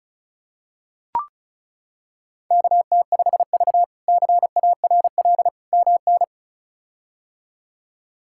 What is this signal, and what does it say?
Morse code, heard as a keyed tone, about 36 wpm, 700 Hz. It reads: KT5V CARL MN
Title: K1USN SST QSO Practice - 35wpm